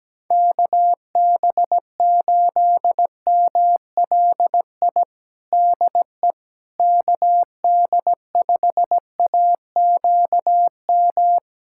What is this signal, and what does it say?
Morse code, heard as a keyed tone, about 17 wpm, 700 Hz. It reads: KB8MLI DE KD5AQM
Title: USA Callsign DE Callsign 17wpm